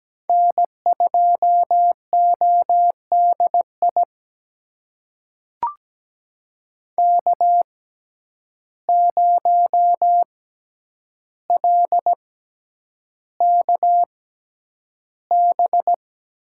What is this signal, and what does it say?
Morse code, heard as a keyed tone, about 17 wpm, spaced as for 9 wpm, 700 Hz. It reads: N2ODI K0LKB
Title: USA Call Signs - 5 Characters - 6x Character Spacing - 17wpm